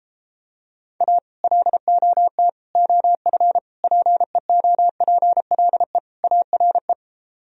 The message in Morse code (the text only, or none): A LOT OF PEOPLE ARE